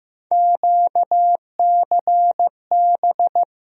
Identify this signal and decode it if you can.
Morse code, heard as a keyed tone, about 15 wpm, 700 Hz. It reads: QCB
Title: USA Call Sign Postfixes - 3 Letters 15wpm